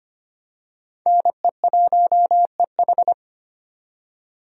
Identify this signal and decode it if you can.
Morse code, heard as a keyed tone, about 25 wpm, 700 Hz. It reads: NE1EH
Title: USA Call Signs - 5 Characters - 25wpm